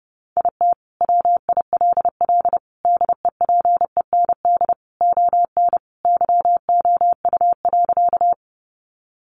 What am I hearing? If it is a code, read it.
Morse code, heard as a keyed tone, about 30 wpm, 700 Hz. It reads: IT WILL DEPEND ON YOU.
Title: Sentences from Top 1300 Words 30wpm